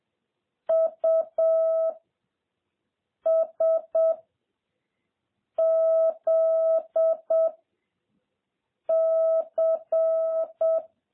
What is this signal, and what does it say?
Morse code, heard as a keyed tone, about 7 wpm, 650 Hz. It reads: U S Z C
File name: jh8cbh / 2021cw